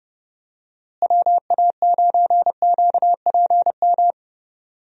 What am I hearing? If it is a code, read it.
Morse code, heard as a keyed tone, about 32 wpm, 700 Hz. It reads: WA9QPM
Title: USA Call Signs - 6 Characters - Repeated 3x - 30wpm